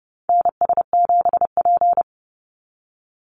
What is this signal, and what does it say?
Morse code, heard as a keyed tone, about 30 wpm, 700 Hz. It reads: NS7P
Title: NA SOTA Activator Callsigns 30wpm